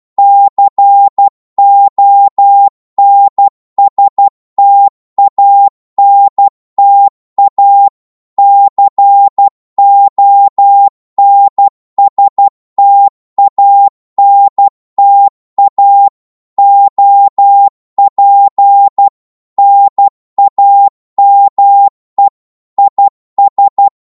Morse code, heard as a keyed tone, about 12 wpm, 800 Hz. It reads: CONSTANTA CONSTANTA OP NAME IS
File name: Morse / qso-12wpm